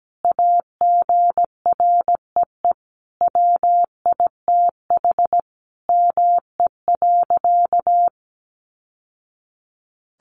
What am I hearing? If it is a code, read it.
Morse code, heard as a keyed tone, about 17 wpm, 700 Hz. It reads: AGREE WITH ME.
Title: Sentences from Top 1100 Words 17wpm